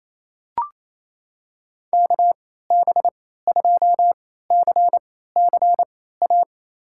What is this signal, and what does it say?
Morse code, heard as a keyed tone, about 28 wpm, spaced as for 16 wpm, 700 Hz. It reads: KB2CCA
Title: USA Call Signs - 6 Characters - 3x Character Spacing - 28wpm